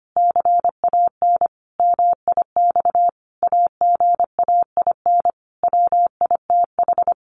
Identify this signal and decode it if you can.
Morse code, heard as a keyed tone, about 25 wpm, 700 Hz. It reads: CAN MIX AGAIN WITH
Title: Sentences from Top 1100 Words 25wpm